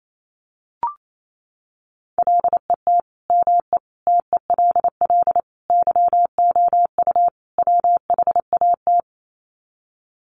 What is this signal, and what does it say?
Morse code, heard as a keyed tone, about 28 wpm, 700 Hz. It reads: LET ME TELL YOU WHAT
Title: Sets of 5 Words Encore 28wpm